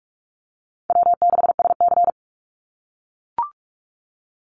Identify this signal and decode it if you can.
Morse code, heard as a keyed tone, about 44 wpm, 700 Hz. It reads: W6SC
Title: Easy Call Signs - 45wpm